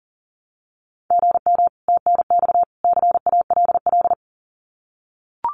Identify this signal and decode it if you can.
Morse code, heard as a keyed tone, about 40 wpm, 700 Hz. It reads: GM TNX CALL
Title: Common QSO Phrases 40wpm